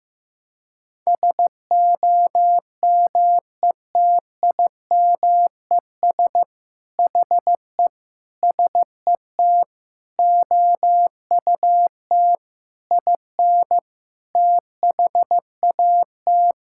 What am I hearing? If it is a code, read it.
Morse code, heard as a keyed tone, about 15 wpm, 700 Hz. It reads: SOMETIMES HE SET OUT IN THAT
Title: Sentences from Top 1000 Words - Review Part 5 - 15wpm